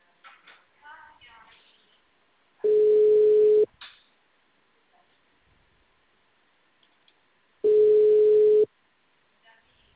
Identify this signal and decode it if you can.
Ringback tone (Europe, 425 Hz), with speech.
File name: out- -2030- - - .208